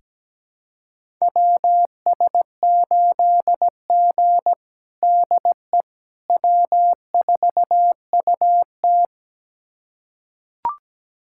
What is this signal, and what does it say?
Morse code, heard as a keyed tone, about 17 wpm, 700 Hz. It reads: WS8G DE W4UT
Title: USA Callsign DE Callsign - 4 Characters - 17wpm